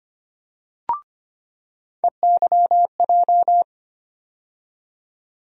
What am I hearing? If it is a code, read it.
Morse code, heard as a keyed tone, about 25 wpm, 700 Hz. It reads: EYJ
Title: USA Call Sign Postfixes - 3 Letters 25wpm